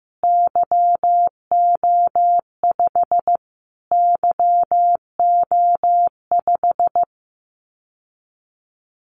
Morse code, heard as a keyed tone, about 15 wpm, 700 Hz. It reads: YO5 YO5
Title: Callsign Prefixes for Top 25 DXCC - Repeated 3x - 15wpm